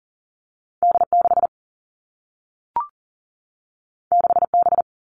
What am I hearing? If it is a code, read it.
Morse code, heard as a keyed tone, about 38 wpm, 700 Hz. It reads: D6 6B
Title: Two Challenging Letter-Number Combinations 40wpm